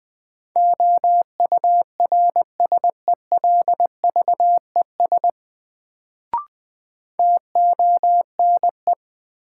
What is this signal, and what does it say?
Morse code, heard as a keyed tone, about 21 wpm, 700 Hz. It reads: OURSELVES TONE
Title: Sentences from Top 1600 Words - New Word Drill - 20wpm